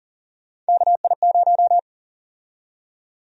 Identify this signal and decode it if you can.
Morse code, heard as a keyed tone, about 40 wpm, 700 Hz. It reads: KI0
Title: USA Call Sign Prefixes - 40wpm